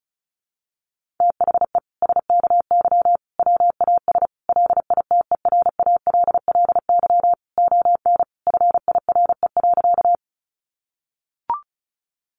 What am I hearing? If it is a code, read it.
Morse code, heard as a keyed tone, about 34 wpm, 700 Hz. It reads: THE SKY WAS LITERALLY ON FIRE.
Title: Sentences from Top 1400 Words 35wpm